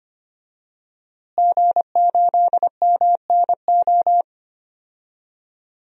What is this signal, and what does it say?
Morse code, heard as a keyed tone, about 25 wpm, 700 Hz. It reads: G8MNO
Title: DXCC-100 Call Signs - 5 Characters - Repeated 2x - 25wpm